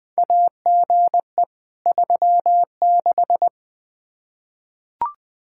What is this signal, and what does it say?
Morse code, heard as a keyed tone, about 20 wpm, 700 Hz. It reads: AGE 36
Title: Practice QSO - Exchange 3 - More Info - 20wpm